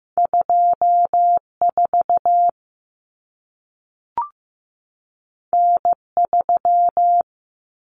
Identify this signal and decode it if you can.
Morse code, heard as a keyed tone, about 15 wpm, 700 Hz. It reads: 24 N3
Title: Common Cut Numbers - 2-Digit 15wpm